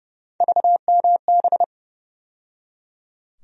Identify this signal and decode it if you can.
Morse code, heard as a keyed tone, about 30 wpm, 700 Hz. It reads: VMB
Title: USA Call Sign Postfixes - 3 Letters 30wpm